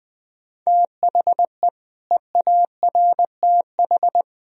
Morse code, heard as a keyed tone, about 21 wpm, 700 Hz. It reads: THE EARTH
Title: Sentences from Top 1000 Words - Review Part 3 - 20wpm